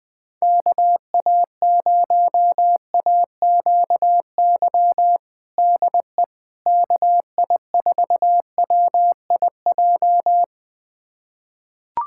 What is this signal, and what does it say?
Morse code, heard as a keyed tone, about 20 wpm, 700 Hz. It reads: KA0AQY DE KI4WIJ
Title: USA Callsign DE Callsign - 6 Characters - 20wpm